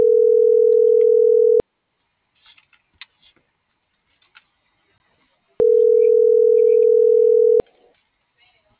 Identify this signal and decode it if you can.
Ringback tone (North America), 440 + 480 Hz.